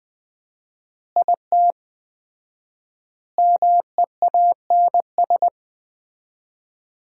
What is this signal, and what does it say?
Morse code, heard as a keyed tone, about 20 wpm, 700 Hz. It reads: IT MEANS
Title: 4x Word-Spacing - Sets of 5 Words - 20wpm